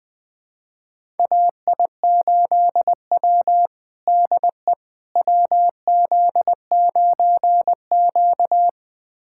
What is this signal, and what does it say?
Morse code, heard as a keyed tone, about 21 wpm, 700 Hz. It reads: AI8W DE WZ9Q
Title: USA Callsign DE Callsign - 4 Characters - 20wpm